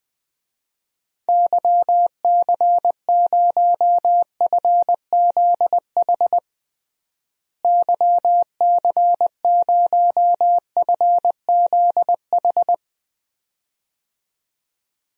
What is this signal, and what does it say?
Morse code, heard as a keyed tone, about 20 wpm, 700 Hz. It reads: YC0FZH YC0FZH
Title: DXCC-100 Call Signs - 6 Characters - Repeated 2x - 20wpm